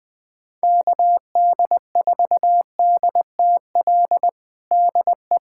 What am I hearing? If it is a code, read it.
Morse code, heard as a keyed tone, about 20 wpm, 700 Hz. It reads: KD4DTL DE
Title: USA Callsign DE Callsign - 6 Characters - 20wpm